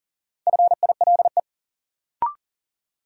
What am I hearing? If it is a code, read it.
Morse code, heard as a keyed tone, about 40 wpm, 700 Hz. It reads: FILE
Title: Sentences from Top 1200 Words - New Word Drill Alt - 40wpm